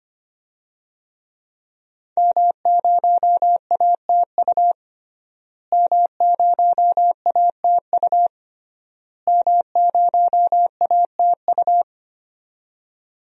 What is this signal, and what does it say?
Morse code, heard as a keyed tone, about 25 wpm, 700 Hz. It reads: M0ATU M0ATU M0ATU
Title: DXCC-100 Call Signs - 5 Characters - Repeated 3x - 25wpm